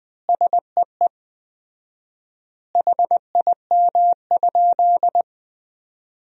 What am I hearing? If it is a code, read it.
Morse code, heard as a keyed tone, about 20 wpm, 700 Hz. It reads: SEE HIM?
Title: 4x Word-Spacing - Sentences from Top 100 Words - 20wpm